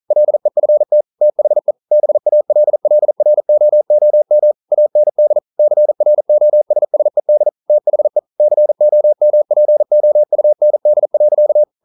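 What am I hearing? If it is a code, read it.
Morse code, heard as a keyed tone, about 42 wpm, 600 Hz. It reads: LEFT THE BALLROOM AND CROSSED THE COMPOUND.
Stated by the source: Morse code practice at 41 wpm, from CW med Gnister.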